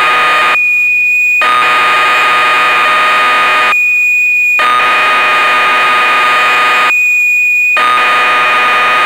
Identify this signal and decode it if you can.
Cassette data (Kansas City type).
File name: magazines / Acorn User / 1985-06 BBC Programs